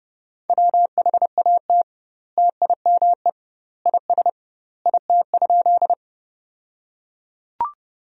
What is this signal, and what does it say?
Morse code, heard as a keyed tone, about 30 wpm, 700 Hz. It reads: WHAT TIME IS IT?
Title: 2x Word-Spacing - Sentences from Top 100 Words - 30wpm